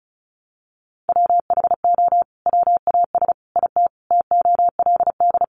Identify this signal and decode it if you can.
Morse code, heard as a keyed tone, about 34 wpm, 700 Hz. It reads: WHO WAS IT TOLD
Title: Sentences from Top 400 Words 35wpm